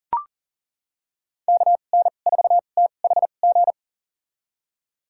Morse code, heard as a keyed tone, about 40 wpm, spaced as for 29 wpm, 700 Hz. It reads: KN4THG
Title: USA Call Signs - 6 Characters - 2x Character Spacing - 40wpm